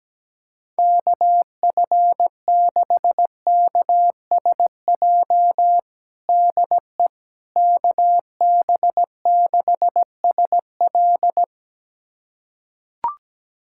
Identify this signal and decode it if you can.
Morse code, heard as a keyed tone, about 17 wpm, 700 Hz. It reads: KF6KSJ DE KB6SL
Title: USA Callsign DE Callsign 17wpm